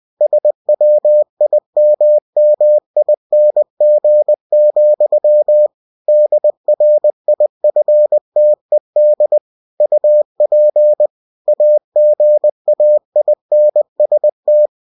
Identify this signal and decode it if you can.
Morse code, heard as a keyed tone, about 20 wpm, 600 Hz. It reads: SWIMMING, DRIFTED UP AGAINST
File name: cw-books / winnie-the-pooh / winnie-20wpm-page-0087